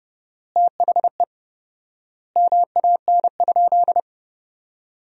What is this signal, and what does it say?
Morse code, heard as a keyed tone, about 32 wpm, 700 Hz. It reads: THE MAN?
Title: 4x Word-Spacing - Sentences from Top 200 Words - 30wpm